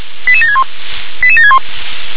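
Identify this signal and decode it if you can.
Five-tone selective call (ZVEI), heard as quick runs of five tones, tones of about 70 ms. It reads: 88761 88761
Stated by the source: ZVEI five-tone selective call received by SDR.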